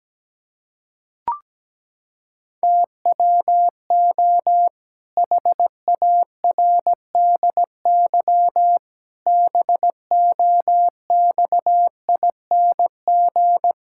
Morse code, heard as a keyed tone, about 17 wpm, 700 Hz. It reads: TWO HARDY BOXING
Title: Pangram Challenge v2 - 17wpm